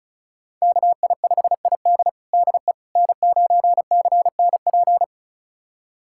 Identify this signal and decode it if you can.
Morse code, heard as a keyed tone, about 36 wpm, 700 Hz. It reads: KI5ID DE N9CNP
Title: USA Callsign DE Callsign - 5 Characters - 35wpm